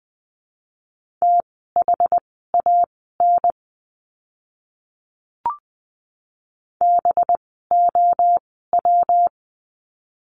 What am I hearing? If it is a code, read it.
Morse code, heard as a keyed tone, about 20 wpm, spaced as for 14 wpm, 700 Hz. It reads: THAN BOW
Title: 185 Lesson 30f - Letter b - Words 20wpm